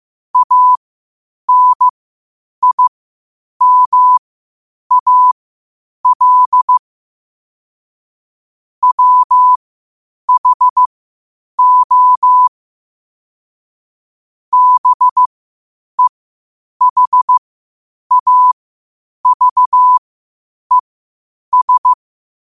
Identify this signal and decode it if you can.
Morse code, heard as a keyed tone, about 15 wpm, spaced as for 8 wpm, 1000 Hz. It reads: ANIMAL WHO BEHAVES
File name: qotdfiles / 8wpm